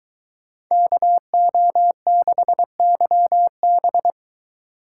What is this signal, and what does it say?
Morse code, heard as a keyed tone, about 22 wpm, 700 Hz. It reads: KO6YB